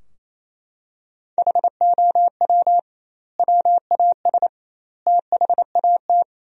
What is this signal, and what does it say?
Morse code, heard as a keyed tone, about 28 wpm, 700 Hz. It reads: HOW WAS THAT
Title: 2x Word-Spacing - Sentences from Top 500 Words - 28wpm